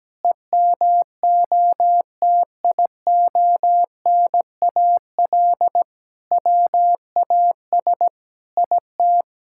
Morse code, heard as a keyed tone, about 17 wpm, 700 Hz. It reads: EMOTIONAL WAS IT
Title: Sentences from Top 1000 Words - Review Part 5 - 17wpm